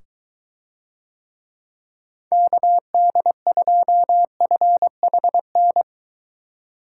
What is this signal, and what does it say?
Morse code, heard as a keyed tone, about 22 wpm, 700 Hz. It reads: KD2FHN